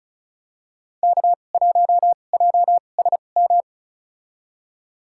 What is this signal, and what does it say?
Morse code, heard as a keyed tone, about 36 wpm, spaced as for 25 wpm, 700 Hz. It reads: K1JSM
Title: USA Call Signs - 5 Characters - 2x Character Spacing - 35wpm